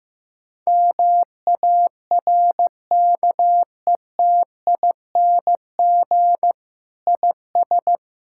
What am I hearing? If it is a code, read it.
Morse code, heard as a keyed tone, about 15 wpm, 700 Hz. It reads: MARKETING IS
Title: Sentences from Top 1000 Words - Review Part 5 - 15wpm